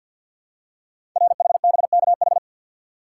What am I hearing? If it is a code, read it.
Morse code, heard as a keyed tone, about 51 wpm, 700 Hz. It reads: RHBKH
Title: Code Groups - 5 Letters - 50wpm